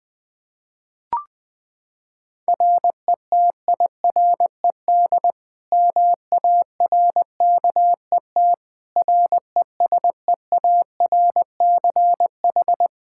Morse code, heard as a keyed tone, about 21 wpm, 700 Hz. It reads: RETIRED MARKET RESEARCH
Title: Practice QSO - Exchange 3 - More Info - 20wpm